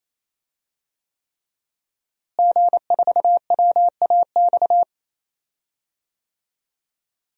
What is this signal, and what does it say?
Morse code, heard as a keyed tone, about 28 wpm, 700 Hz. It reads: G4WAX
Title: DXCC-100 Call Signs - 5 Characters - Repeated 3x - 28wpm